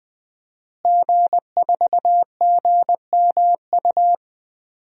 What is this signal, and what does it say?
Morse code, heard as a keyed tone, about 19 wpm, 700 Hz. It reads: G4GMU
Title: DXCC-100 Call Signs - 5 Characters - Repeated 3x - 20wpm